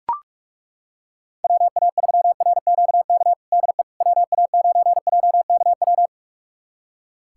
Morse code, heard as a keyed tone, about 41 wpm, 700 Hz. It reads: WA3RQK DE WA9JKW
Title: USA Callsign DE Callsign - 6 Characters - 45wpm